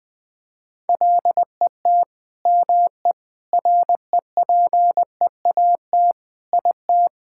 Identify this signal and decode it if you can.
Morse code, heard as a keyed tone, about 21 wpm, 700 Hz. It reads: LET ME REPEAT IT
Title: Sentences from Top 1600 Words 20wpm